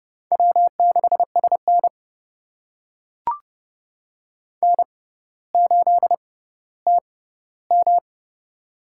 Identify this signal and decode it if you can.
Morse code, heard as a keyed tone, about 30 wpm, spaced as for 15 wpm, 700 Hz. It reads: W6SN N8TM
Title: USA Call Signs - 6x Character Spacing - 30wpm